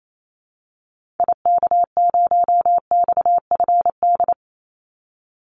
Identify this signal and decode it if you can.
Morse code, heard as a keyed tone, about 28 wpm, 700 Hz. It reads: IK0XFD
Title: Call Signs - 28wpm